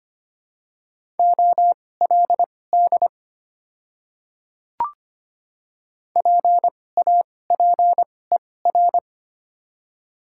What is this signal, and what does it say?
Morse code, heard as a keyed tone, about 25 wpm, spaced as for 18 wpm, 690 Hz. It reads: OLD PAPER
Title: 2x Character Spacing - Top 300 Words - 25wpm